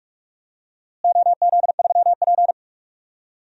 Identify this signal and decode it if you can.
Morse code, heard as a keyed tone, about 46 wpm, 700 Hz. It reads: OZ3P
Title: DXCC-100 Call Signs - 4 Characters - Repeated 3x - 45wpm